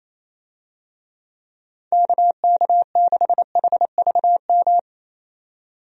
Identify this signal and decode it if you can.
Morse code, heard as a keyed tone, about 28 wpm, 700 Hz. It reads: KK6HVM
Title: USA Call Signs - 6 Characters - 28wpm